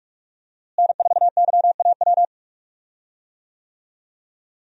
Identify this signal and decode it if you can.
Morse code, heard as a keyed tone, about 46 wpm, 700 Hz. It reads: N4YAW
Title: USA Call Signs - 5 Characters - Repeated 3x - 45wpm